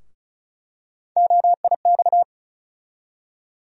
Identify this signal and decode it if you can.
Morse code, heard as a keyed tone, about 36 wpm, 700 Hz. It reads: OIX
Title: USA Call Sign Postfixes - 3 Letters 35wpm